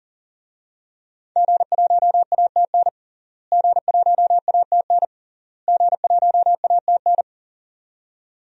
Morse code, heard as a keyed tone, about 40 wpm, 700 Hz. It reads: G1ATN G1ATN G1ATN
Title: DXCC-100 Call Signs - 5 Characters - Repeated 3x - 40wpm